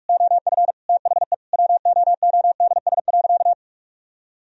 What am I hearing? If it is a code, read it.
Morse code, heard as a keyed tone, about 42 wpm, 700 Hz. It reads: OF THE WOODS.
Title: Sentences from Top 1000 Words - Review Part 2 - 45wpm